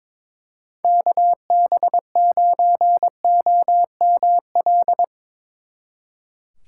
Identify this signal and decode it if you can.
Morse code, heard as a keyed tone, about 24 wpm, 700 Hz. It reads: KB9OML
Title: Speed Racing - USA 2x3 Call Signs 22wpm